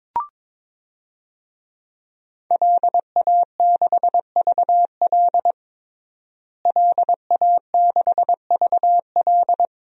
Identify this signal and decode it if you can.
Morse code, heard as a keyed tone, about 22 wpm, 700 Hz. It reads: LA6VL LA6VL
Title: DXCC-100 Call Signs - 5 Characters - Repeated 3x - 22wpm